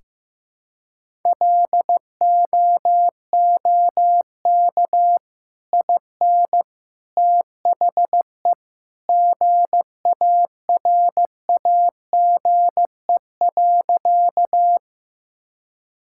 Morse code, heard as a keyed tone, about 16 wpm, 700 Hz. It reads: LOOK IN THE GARAGE.
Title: Sentences from Top 1600 Words 15wpm